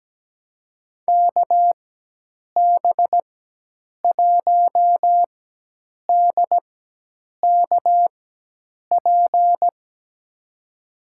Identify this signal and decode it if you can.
Morse code, heard as a keyed tone, about 17 wpm, spaced as for 8 wpm, 700 Hz. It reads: KB1DKP
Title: USA Call Signs - 6 Characters - 4x Character Spacing - 17wpm